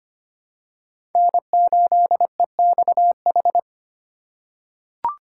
Morse code, heard as a keyed tone, about 25 wpm, 700 Hz. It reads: N8EXH
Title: USA Call Signs - 5 Characters - 4x Character Spacing - 25wpm